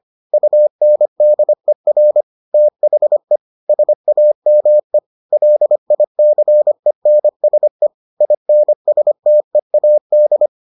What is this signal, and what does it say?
Morse code, heard as a keyed tone, about 25 wpm, 600 Hz. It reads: UNDER THE SAME LICENSE INSTEAD